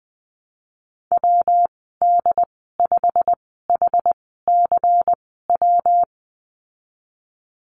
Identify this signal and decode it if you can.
Morse code, heard as a keyed tone, about 20 wpm, spaced as for 15 wpm, 700 Hz. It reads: WD5HCW
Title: 128 Lesson 21g - Number 6 - Call Signs 20wpm